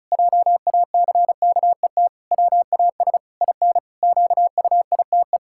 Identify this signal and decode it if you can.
Morse code, heard as a keyed tone, about 36 wpm, 700 Hz. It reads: JACKET WAS IN QUITE
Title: Pangram Challenge v2 - 35wpm